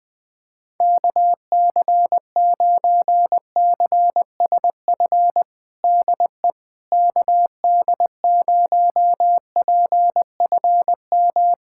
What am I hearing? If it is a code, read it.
Morse code, heard as a keyed tone, about 21 wpm, 700 Hz. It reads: KC9CSF DE KD0PFM
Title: USA Callsign DE Callsign - 6 Characters - 20wpm